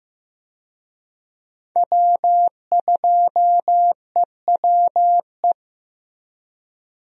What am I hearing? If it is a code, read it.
Morse code, heard as a keyed tone, about 14 wpm, 700 Hz. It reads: W2EWE
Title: USA Call Signs - 5 Characters - Repeated 3x - 15wpm